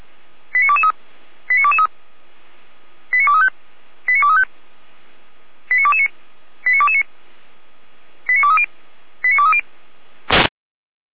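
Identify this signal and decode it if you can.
Five-tone selective call (ZVEI), heard as quick runs of five tones, tones of about 70 ms. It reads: 89223 89223 89235 89235 89229 89229 89230 89230
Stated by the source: ZVEI five-tone selective call received by SDR.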